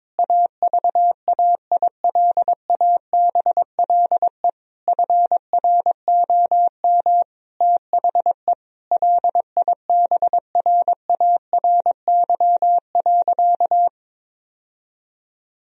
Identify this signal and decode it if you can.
Morse code, heard as a keyed tone, about 22 wpm, 700 Hz. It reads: AVAILABLE FROM THE LIBRARY.
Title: Sentences from Top 1000 Words - Review Part 3 - 22wpm